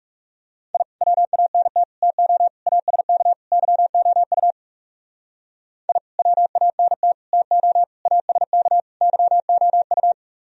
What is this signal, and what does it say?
Morse code, heard as a keyed tone, about 38 wpm, 700 Hz. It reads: I WANT TO ASK YOU I WANT TO ASK YOU